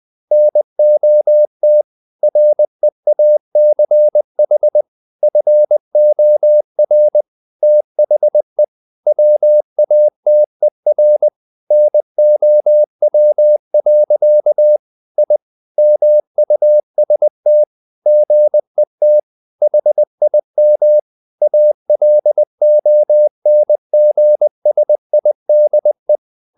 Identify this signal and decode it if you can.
Morse code, heard as a keyed tone, about 20 wpm, 600 Hz. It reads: NOT REACH FOR THE WATER NOW. I MUST GET HIM ALONGSIDE
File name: the old man and the sea part 2 20wpm